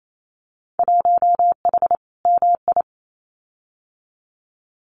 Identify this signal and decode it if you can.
Morse code, heard as a keyed tone, about 28 wpm, 700 Hz. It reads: 1H MI
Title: Winter Field Day Exchanges 28wpm